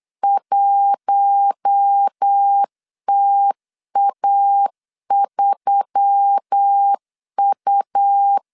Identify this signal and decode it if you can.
Morse code, heard as a keyed tone, about 8 wpm, 800 Hz. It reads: 1TA3U
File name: LEZIONI / 8 YJ45 / 8 YJ45 8wpm 3214